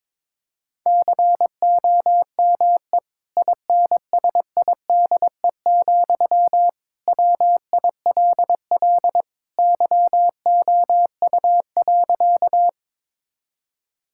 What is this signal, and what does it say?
Morse code, heard as a keyed tone, about 22 wpm, 700 Hz. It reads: COME INSIDE, WILL YOU.
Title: Sentences from Top 1000 Words - Review Part 6 - 22wpm